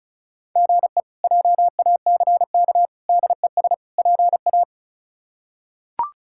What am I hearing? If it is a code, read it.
Morse code, heard as a keyed tone, about 36 wpm, 700 Hz. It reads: GE JACK DES PA
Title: K1USN SST QSO Practice - 35wpm